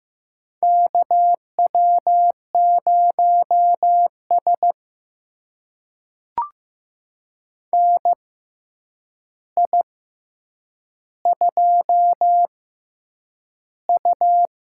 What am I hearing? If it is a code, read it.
Morse code, heard as a keyed tone, about 15 wpm, spaced as for 8 wpm, 700 Hz. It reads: KW0S NI2U
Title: USA Call Signs - 4 Characters - 6x Character Spacing - 15wpm